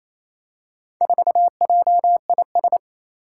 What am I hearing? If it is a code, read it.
Morse code, heard as a keyed tone, about 28 wpm, 700 Hz. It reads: 4JIS
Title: Four Challenging Letter-Number Combinations 28wpm